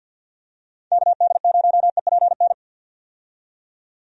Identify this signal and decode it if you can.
Morse code, heard as a keyed tone, about 47 wpm, 700 Hz. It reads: KD0EPN
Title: USA Call Signs - 6 Characters - 50wpm